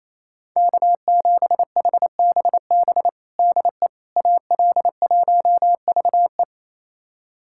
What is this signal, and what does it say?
Morse code, heard as a keyed tone, about 28 wpm, 700 Hz. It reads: K7HBB DE AL1VE
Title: USA Callsign DE Callsign - 5 Characters - 28wpm